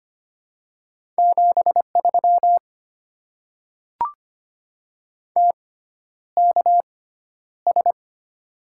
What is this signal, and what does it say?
Morse code, heard as a keyed tone, about 25 wpm, spaced as for 11 wpm, 700 Hz. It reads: 73 TKS
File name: Top 90 QSO Elements - 6x Character Spacing - 25wpm